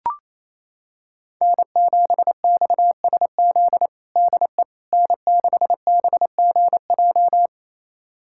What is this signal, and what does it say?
Morse code, heard as a keyed tone, about 28 wpm, 700 Hz. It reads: N7XSZ DE N6BGJ
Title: USA Callsign DE Callsign - 5 Characters - 28wpm